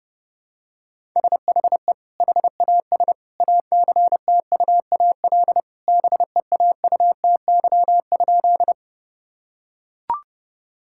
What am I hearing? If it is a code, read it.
Morse code, heard as a keyed tone, about 31 wpm, 700 Hz. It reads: SHE HAS ACTUAL BEAUTY?
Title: Sentences from Top 1000 Words - Review Part 5 - 30wpm